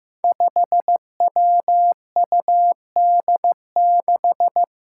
Morse code, heard as a keyed tone, about 15 wpm, 700 Hz. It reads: 5WUD6
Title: Five Challenging Letter-Number Combinations 15wpm